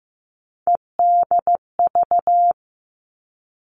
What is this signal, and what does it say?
Morse code, heard as a keyed tone, about 15 wpm, 700 Hz. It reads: EDV
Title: All Cut Numbers - 3-Digit 15wpm